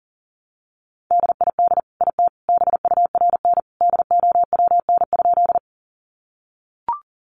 Morse code, heard as a keyed tone, about 40 wpm, 700 Hz. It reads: DID IT BURN DOWN?
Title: Sentences from Top 1300 Words 40wpm